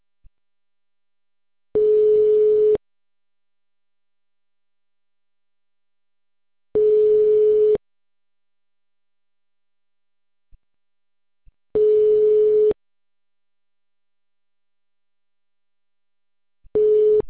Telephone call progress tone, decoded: ringback tone

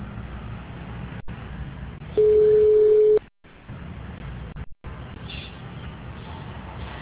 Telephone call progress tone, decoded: ringback tone

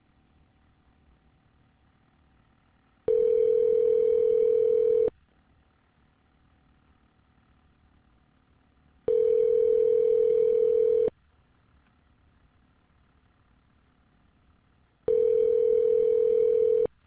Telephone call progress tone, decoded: ringback tone

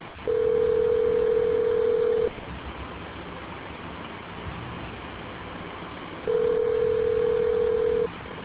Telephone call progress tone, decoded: ringback tone

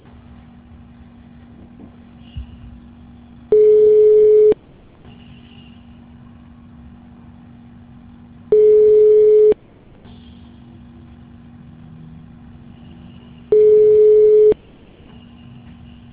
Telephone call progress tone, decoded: ringback tone

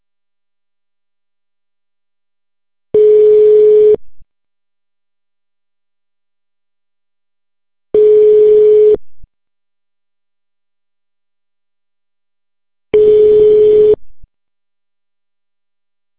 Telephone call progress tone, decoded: ringback tone

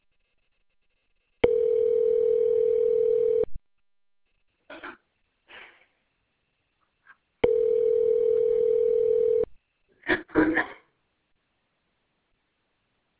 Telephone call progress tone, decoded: ringback tone

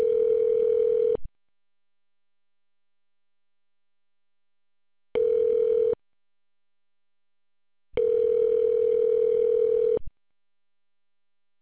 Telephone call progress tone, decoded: ringback tone

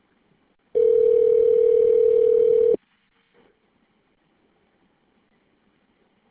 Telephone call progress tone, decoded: ringback tone

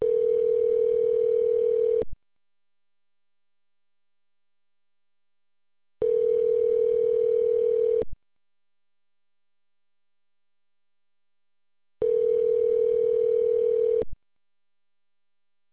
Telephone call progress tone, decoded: ringback tone

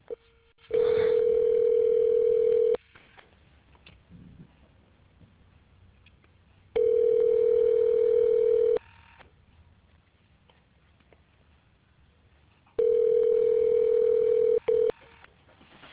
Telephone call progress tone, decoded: ringback tone

